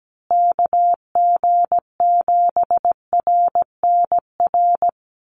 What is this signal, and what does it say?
Morse code, heard as a keyed tone, about 17 wpm, 700 Hz. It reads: KG7RNR
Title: USA Call Signs Repeated 2x 17wpm